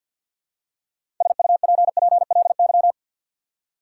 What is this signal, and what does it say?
Morse code, heard as a keyed tone, about 50 wpm, 700 Hz. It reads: SUPPLY
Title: Six Letter Words - 50wpm